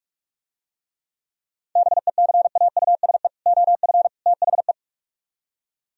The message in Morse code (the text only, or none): BECAUSE OF THE